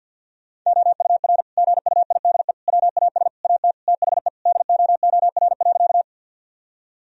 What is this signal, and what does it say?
Morse code, heard as a keyed tone, about 50 wpm, 700 Hz. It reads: OUR GUIDE WAS AT THE DOOR.
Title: Sentences from Top 1000 Words - Review Part 3 - 50wpm